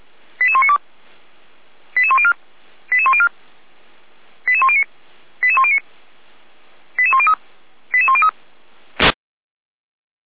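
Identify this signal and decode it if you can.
Five-tone selective call (ZVEI), heard as quick runs of five tones, tones of about 70 ms. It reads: 88192 88104 88104 88109 88109 88103 88103
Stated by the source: Received on an SDR; ZVEI five-tone selective call.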